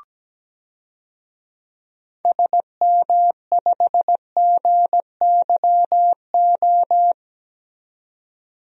Morse code, heard as a keyed tone, about 17 wpm, 700 Hz. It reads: SM5GYO
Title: DXCC-100 Call Signs - 6 Characters - 17wpm